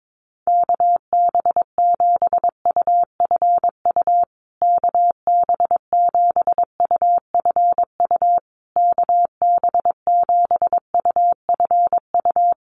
Morse code, heard as a keyed tone, about 22 wpm, 700 Hz. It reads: KB7UFU KB7UFU KB7UFU
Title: USA Call Signs Repeated 3x 22wpm